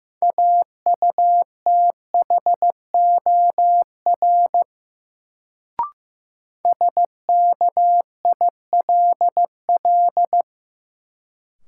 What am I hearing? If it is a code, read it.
Morse code, heard as a keyed tone, about 15 wpm, 700 Hz. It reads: AUTHOR SKILL
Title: Sentences from Top 1200 Words - New Word Drill Alt - 15wpm